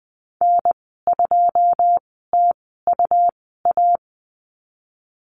Morse code, heard as a keyed tone, about 20 wpm, spaced as for 14 wpm, 700 Hz. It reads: N2TUA
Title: 128 Lesson 21g - Number 6 - Call Signs 20wpm